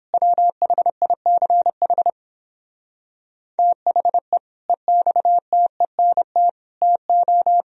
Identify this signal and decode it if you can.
Morse code, heard as a keyed tone, about 31 wpm, 700 Hz. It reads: WHICH THE EXTENT TO